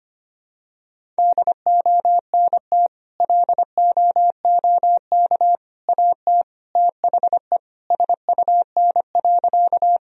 Morse code, heard as a keyed tone, about 25 wpm, 700 Hz. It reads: DONT LOOK AT THE SUN.
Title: Sentences from Top 1000 Words - Review Part 1 - 25wpm